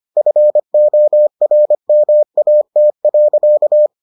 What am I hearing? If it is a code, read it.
Morse code, heard as a keyed tone, about 25 wpm, 600 Hz. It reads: FORMAT.